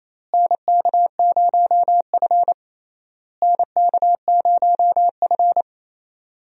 Morse code, heard as a keyed tone, about 28 wpm, 700 Hz. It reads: NK0F NK0F
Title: USA Call Signs - 4 Characters - Repeated 3x - 28wpm